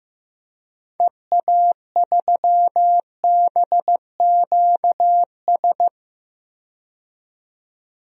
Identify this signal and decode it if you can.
Morse code, heard as a keyed tone, about 15 wpm, 700 Hz. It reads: EA3BQS
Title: DXCC-100 Call Signs - 6 Characters - 15wpm